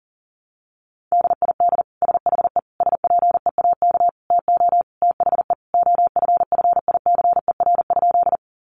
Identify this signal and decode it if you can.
Morse code, heard as a keyed tone, about 40 wpm, 700 Hz. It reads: DID SHE SPEAK TO THE OFFICER?
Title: Sentences from Top 1200 Words 40wpm